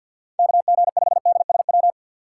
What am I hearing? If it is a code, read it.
Morse code, heard as a keyed tone, about 46 wpm, 700 Hz. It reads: KG5DSW
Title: USA Call Signs - 6 Characters - 6x Character Spacing - 50wpm